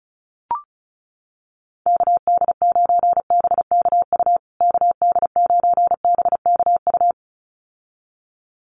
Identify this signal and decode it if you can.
Morse code, heard as a keyed tone, about 34 wpm, 700 Hz. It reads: KD9BKU KD9BKU
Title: USA Call Signs Repeated 2x 35wpm